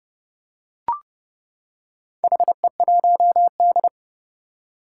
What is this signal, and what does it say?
Morse code, heard as a keyed tone, about 31 wpm, 700 Hz. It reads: HE1D
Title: Four Challenging Letter-Number Combinations 30wpm